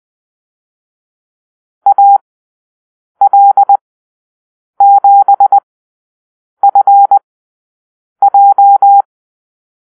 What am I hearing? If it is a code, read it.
Morse code, heard as a keyed tone, about 20 wpm, spaced as for 7 wpm, 800 Hz. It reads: AL7FJ
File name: arrl / Call-Sign Drill-8